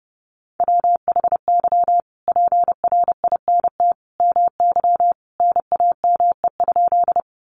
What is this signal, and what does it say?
Morse code, heard as a keyed tone, about 30 wpm, 700 Hz. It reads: WHY PRINT MY NAME?
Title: Sentences from Top 1400 Words 30wpm